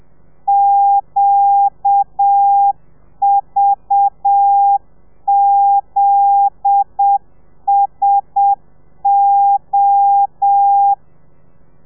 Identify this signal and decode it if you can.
Morse code, heard as a keyed tone, about 7 wpm, 800 Hz. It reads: QVZSO